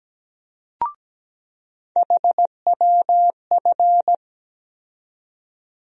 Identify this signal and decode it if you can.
Morse code, heard as a keyed tone, about 17 wpm, 700 Hz. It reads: HWF
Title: USA Call Sign Postfixes - 3 Letters 17wpm